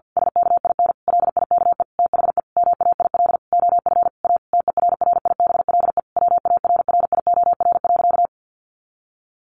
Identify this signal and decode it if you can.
Morse code, heard as a keyed tone, about 48 wpm, 700 Hz. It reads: SKIN LIKE THE MAIL OF A TERRIBLE WARRIOR.
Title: A Sound Of Thunder 50wpm